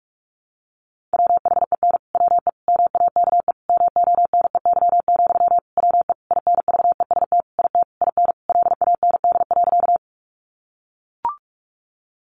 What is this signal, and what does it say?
Morse code, heard as a keyed tone, about 44 wpm, 700 Hz. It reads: WHEN WE MAKE MONEY, WE INVEST IT IN LAND.
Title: Sentences from Top 1100 Words 45wpm